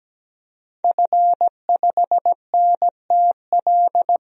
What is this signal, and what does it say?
Morse code, heard as a keyed tone, about 17 wpm, 700 Hz. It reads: F5NTL
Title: DXCC-100 Call Signs - 5 Characters - Repeated 2x - 17wpm